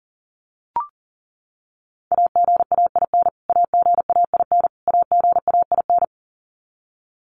Morse code, heard as a keyed tone, about 40 wpm, 700 Hz. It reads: AGAIN AGAIN AGAIN
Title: Top 500 Words Repeated 40wpm